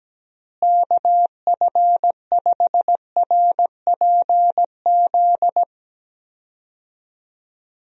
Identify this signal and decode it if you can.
Morse code, heard as a keyed tone, about 17 wpm, 700 Hz. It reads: KF5RPZ
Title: USA Call Signs - 6 Characters - Repeated 2x - 17wpm